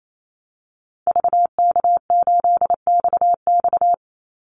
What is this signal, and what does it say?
Morse code, heard as a keyed tone, about 28 wpm, 700 Hz. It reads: VK8XX
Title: Call Signs - 28wpm